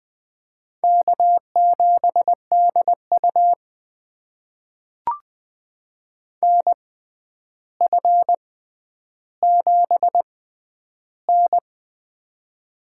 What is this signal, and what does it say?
Morse code, heard as a keyed tone, about 20 wpm, spaced as for 10 wpm, 700 Hz. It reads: K7DU NF7N
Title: USA Call Signs - 4 Characters - 6x Character Spacing - 20wpm